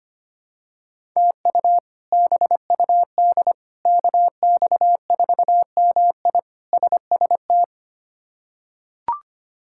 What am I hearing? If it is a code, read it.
Morse code, heard as a keyed tone, about 25 wpm, 700 Hz. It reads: TU BUD KX4MI SST
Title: K1USN SST QSO Practice - 25wpm